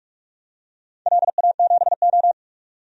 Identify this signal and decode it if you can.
Morse code, heard as a keyed tone, about 47 wpm, 700 Hz. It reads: LA7O
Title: DXCC-100 Call Signs - 4 Characters - 45wpm